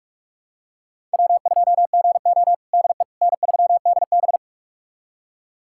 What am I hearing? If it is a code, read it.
Morse code, heard as a keyed tone, about 46 wpm, 700 Hz. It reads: W2GO DE N3DB
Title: USA Callsign DE Callsign - 4 Characters - 45wpm